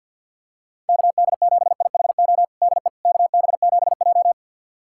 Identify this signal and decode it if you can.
Morse code, heard as a keyed tone, about 51 wpm, 700 Hz. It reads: KD7IHO DE KB7J
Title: USA Callsign DE Callsign 50wpm